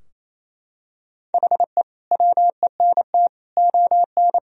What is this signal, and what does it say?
Morse code, heard as a keyed tone, about 28 wpm, 700 Hz. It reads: HE WENT ON
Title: Sentences from Top 1000 Words - Review Part 3 - 28wpm